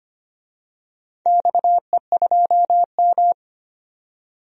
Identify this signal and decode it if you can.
Morse code, heard as a keyed tone, about 25 wpm, 700 Hz. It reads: XE2M
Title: DXCC-100 Call Signs - 4 Characters - 25wpm